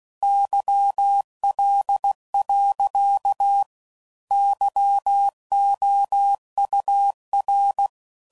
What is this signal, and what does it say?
Morse code, heard as a keyed tone, about 16 wpm, 800 Hz. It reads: YL. YOUR
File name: CW Code Groups @15WPM / QSOs / WA7QOX QRX